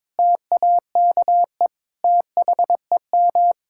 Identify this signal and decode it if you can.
Morse code, heard as a keyed tone, about 22 wpm, 700 Hz. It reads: TAKE THEM
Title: Sentences from Top 1600 Words 22wpm